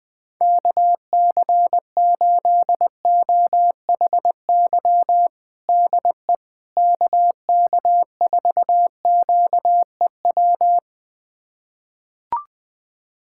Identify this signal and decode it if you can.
Morse code, heard as a keyed tone, about 20 wpm, 700 Hz. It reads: KC8OHY DE KK4QEW
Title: USA Callsign DE Callsign 20wpm